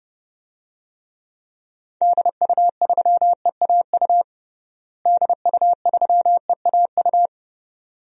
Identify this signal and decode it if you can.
Morse code, heard as a keyed tone, about 32 wpm, 700 Hz. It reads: DU3EAU DU3EAU
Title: DXCC-100 Call Signs - 6 Characters - Repeated 2x - 30wpm